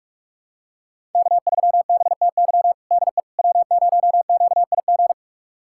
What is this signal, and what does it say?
Morse code, heard as a keyed tone, about 46 wpm, 700 Hz. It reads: K3BTY DE W0QIG
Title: USA Callsign DE Callsign - 5 Characters - 45wpm